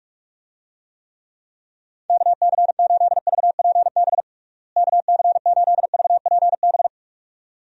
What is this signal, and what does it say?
Morse code, heard as a keyed tone, about 46 wpm, 700 Hz. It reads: KC8VPB KC8VPB
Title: USA Call Signs - 6 Characters - Repeated 2x - 45wpm